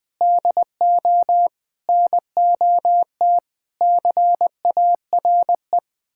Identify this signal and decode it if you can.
Morse code, heard as a keyed tone, about 20 wpm, 700 Hz. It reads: DO NOT CARE
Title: Build A Phrase - 5-Word Phrases - 20wpm